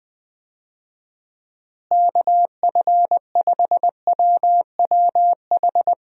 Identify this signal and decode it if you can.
Morse code, heard as a keyed tone, about 20 wpm, 700 Hz. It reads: KF5WWH
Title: USA Call Signs - 6 Characters - Repeated 2x - 20wpm